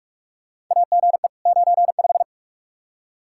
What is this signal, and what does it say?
Morse code, heard as a keyed tone, about 45 wpm, 700 Hz. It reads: AGE 95
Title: K1USN SST QSO Practice 45wpm